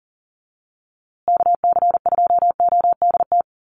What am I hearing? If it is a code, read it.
Morse code, heard as a keyed tone, about 40 wpm, 700 Hz. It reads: KC2ODT
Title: USA Call Signs Repeated 2x 40wpm